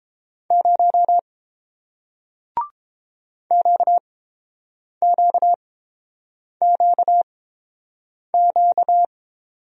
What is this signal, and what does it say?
Morse code, heard as a keyed tone, about 30 wpm, spaced as for 7 wpm, 700 Hz. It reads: 0 QQQQ